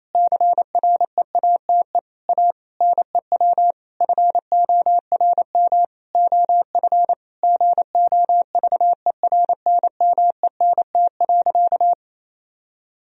Morse code, heard as a keyed tone, about 28 wpm, 700 Hz. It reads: CREATE A NEW FORM OF GOVERNMENT.
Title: Sentences from Top 1000 Words - Review Part 3 - 28wpm